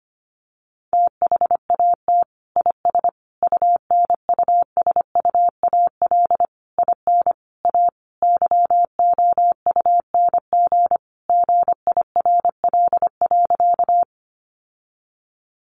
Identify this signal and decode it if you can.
Morse code, heard as a keyed tone, about 25 wpm, 700 Hz. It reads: THAT IS UNUSUAL IN A YOUNG GIRL.
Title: Sentences from Top 1400 Words 25wpm